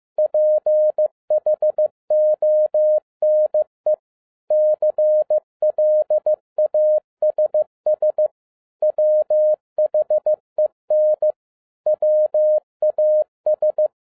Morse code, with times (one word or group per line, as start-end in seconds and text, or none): 0.2-3.9 PHONE
4.5-8.3 CLASS
8.8-11.3 WHEN
11.9-13.9 WAS